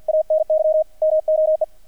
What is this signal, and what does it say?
Morse code, heard as a keyed tone, about 48 wpm, 620 Hz, with radio noise. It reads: ANY MOE
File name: archived / CRSnet